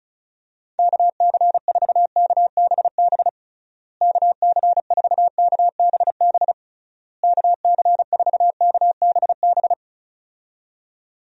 Morse code, heard as a keyed tone, about 34 wpm, 700 Hz. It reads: KC4KBB KC4KBB KC4KBB
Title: USA Call Signs - 6 Characters - Repeated 3x - 35wpm